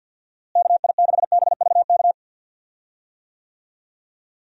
Morse code, heard as a keyed tone, about 45 wpm, 700 Hz. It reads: KI6BVK
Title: USA Call Signs - 6 Characters - Repeated 3x - 50wpm